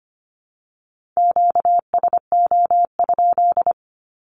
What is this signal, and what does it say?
Morse code, heard as a keyed tone, about 25 wpm, 700 Hz. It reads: QSO?
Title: Q-Codes 25wpm